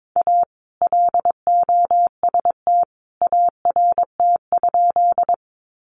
Morse code, heard as a keyed tone, about 22 wpm, 700 Hz. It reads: A LOST ART?
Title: Sentences from Top 1400 Words 22wpm